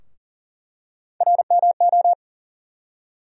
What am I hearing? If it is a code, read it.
Morse code, heard as a keyed tone, about 37 wpm, 700 Hz. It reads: RMO
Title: USA Call Sign Postfixes - 3 Letters 40wpm